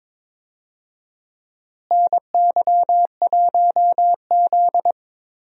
Morse code, heard as a keyed tone, about 22 wpm, 700 Hz. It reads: NY1Z
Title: USA Call Signs - 4 Characters - Repeated 3x - 22wpm